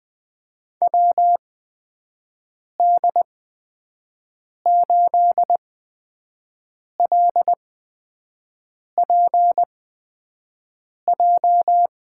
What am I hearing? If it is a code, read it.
Morse code, heard as a keyed tone, about 20 wpm, spaced as for 5 wpm, 700 Hz. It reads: WD8LPJ
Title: USA Call Signs - 6 Characters - 8x Character Spacing - 20wpm